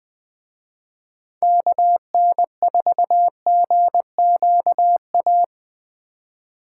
Morse code, heard as a keyed tone, about 20 wpm, 700 Hz. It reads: KN4GQA